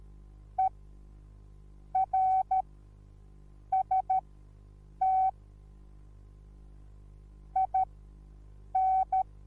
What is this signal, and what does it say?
Morse code, heard as a keyed tone, about 13 wpm, spaced as for 6 wpm, 760 Hz. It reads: ERST IN